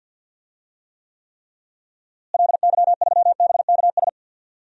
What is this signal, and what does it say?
Morse code, heard as a keyed tone, about 51 wpm, 700 Hz. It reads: LY3BKS
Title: DXCC-100 Call Signs - 6 Characters - Repeated 3x - 50wpm